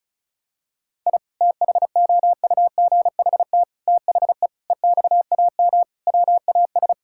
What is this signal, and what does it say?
Morse code, heard as a keyed tone, about 36 wpm, 700 Hz. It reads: I THOUGHT THE EXAM WAS
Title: Sentences from Top 1000 Words - Review Part 5 - 35wpm